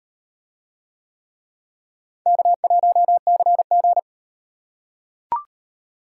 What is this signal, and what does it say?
Morse code, heard as a keyed tone, about 35 wpm, 700 Hz. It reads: K1CG